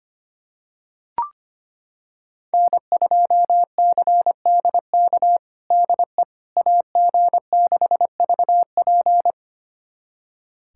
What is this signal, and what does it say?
Morse code, heard as a keyed tone, about 25 wpm, 700 Hz. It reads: N2CDK DE AG6VP
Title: USA Callsign DE Callsign - 5 Characters - 25wpm